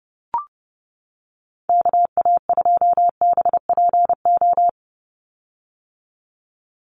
Morse code, heard as a keyed tone, about 30 wpm, 700 Hz. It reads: KA2BPO
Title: USA Call Signs 30wpm